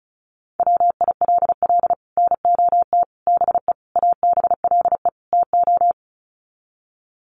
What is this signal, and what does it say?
Morse code, heard as a keyed tone, about 34 wpm, 700 Hz. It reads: WILL NOT BE ABLE TO
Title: Sets of 5 Words 35wpm